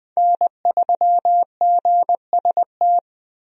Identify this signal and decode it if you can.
Morse code, heard as a keyed tone, about 20 wpm, 700 Hz. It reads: N3GST
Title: USA Call Signs - 5 Characters - 20wpm